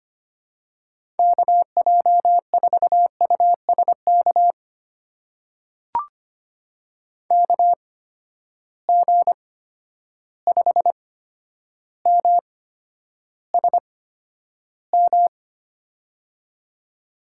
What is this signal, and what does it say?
Morse code, heard as a keyed tone, about 25 wpm, spaced as for 11 wpm, 700 Hz. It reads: KJ4USK KG5MSM
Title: USA Call Signs - 6 Characters - 8x Character Spacing - 25wpm